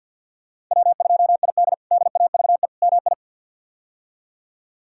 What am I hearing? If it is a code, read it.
Morse code, heard as a keyed tone, about 47 wpm, 700 Hz. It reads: W2ID DAVE MI
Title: K1USN SST QSO Practice - 50wpm